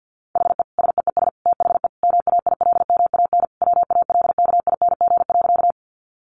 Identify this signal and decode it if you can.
Morse code, heard as a keyed tone, about 50 wpm, 700 Hz. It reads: HE SEES THE MAILMAN WALKING.
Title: Easy Phrases 50wpm